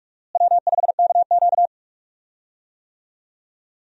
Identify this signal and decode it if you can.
Morse code, heard as a keyed tone, about 44 wpm, 700 Hz. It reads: W5KQ
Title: USA Call Signs - 4 Characters - Repeated 3x - 45wpm